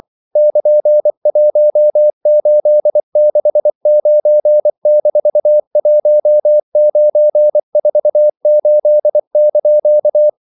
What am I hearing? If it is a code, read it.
Morse code, heard as a keyed tone, about 24 wpm, 600 Hz. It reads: <KN>1869-1948)
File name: CW / Practice / EnglishQuotations0224